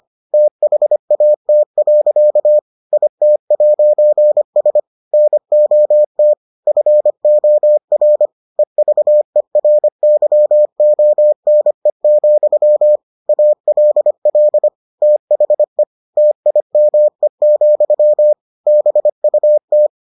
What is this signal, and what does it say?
Morse code, heard as a keyed tone, about 25 wpm, 600 Hz. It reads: THAT. IT'S NOT FOR EVERYONE, ALL THE TIME, BUT